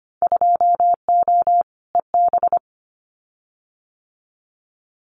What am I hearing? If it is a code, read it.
Morse code, heard as a keyed tone, about 25 wpm, 700 Hz. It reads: 2O EB
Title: Winter Field Day Exchanges Encore 25wpm